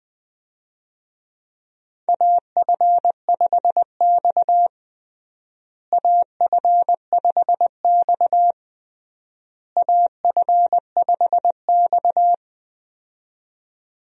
Morse code, heard as a keyed tone, about 21 wpm, 700 Hz. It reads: AF5X AF5X AF5X
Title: USA Call Signs - 4 Characters - Repeated 3x - 20wpm